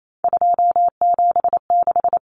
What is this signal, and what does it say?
Morse code, heard as a keyed tone, about 28 wpm, 700 Hz. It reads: 276
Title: Common Cut Numbers - 3-Digit 28wpm